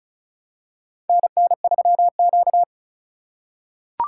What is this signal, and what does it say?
Morse code, heard as a keyed tone, about 34 wpm, 700 Hz. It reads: NN3Q
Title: USA Call Signs - 4x Character Spacing - 35wpm